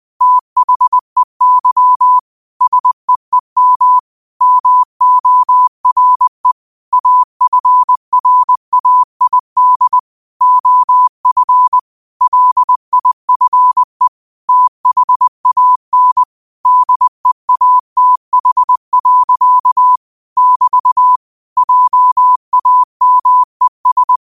Morse code, heard as a keyed tone, about 20 wpm, 1000 Hz. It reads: THEY SEEM MORE AFRAID OF LIFE THAN DEATH. <BT> JAMES